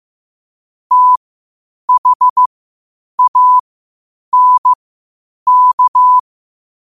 Morse code, heard as a keyed tone, about 15 wpm, spaced as for 8 wpm, 1000 Hz. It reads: THANK